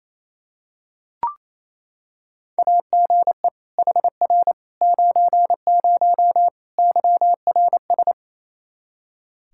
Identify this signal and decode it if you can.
Morse code, heard as a keyed tone, about 28 wpm, 700 Hz. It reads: AGE HR 90 YRS
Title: Practice QSO - Exchange 3 - More Info - 28wpm